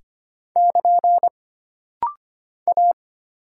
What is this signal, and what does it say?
Morse code, heard as a keyed tone, about 25 wpm, 700 Hz. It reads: <KN> A
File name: Single Character (Letter:Number:Punctuation:Prosign) - Rapid-Fire - 25wpm